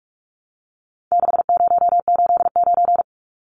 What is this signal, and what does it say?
Morse code, heard as a keyed tone, about 44 wpm, 700 Hz. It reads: 6089
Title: Numbers - 4-Digit 45wpm